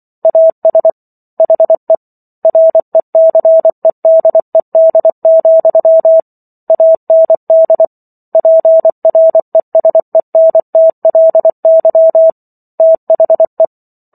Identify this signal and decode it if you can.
Morse code, heard as a keyed tone, about 24 wpm, 650 Hz. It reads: AS HE RECEDED, AND PRESENTLY THE
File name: Book One / 15-Chapter 15 - 24 WPM